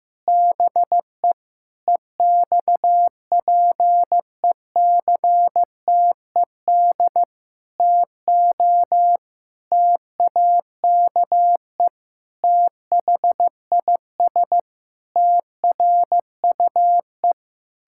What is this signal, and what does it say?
Morse code, heard as a keyed tone, about 15 wpm, 700 Hz. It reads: BE EXPECTED TO TAKE THIS TRUE
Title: Pangram Challenge v2 - 15wpm